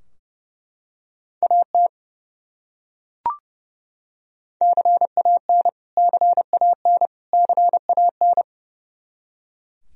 Morse code, heard as a keyed tone, about 32 wpm, 700 Hz. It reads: AT CAN CAN CAN
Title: Top 100 Words Repeated 30wpm